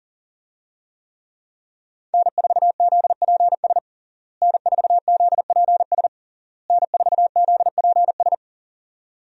Morse code, heard as a keyed tone, about 40 wpm, 700 Hz. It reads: N4ZPS N4ZPS N4ZPS
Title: USA Call Signs - 5 Characters - Repeated 3x - 40wpm